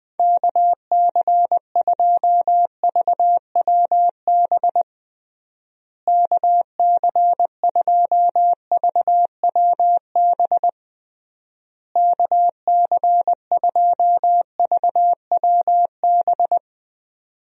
Morse code, heard as a keyed tone, about 21 wpm, 700 Hz. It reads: KC2VWB KC2VWB KC2VWB
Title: USA Call Signs - 6 Characters - Repeated 3x - 20wpm